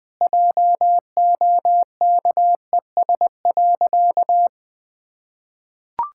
Morse code, heard as a keyed tone, about 20 wpm, 700 Hz. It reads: JOKES.
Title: Sentences with 2-Word Phrases - Spoken Beginning - 20wpm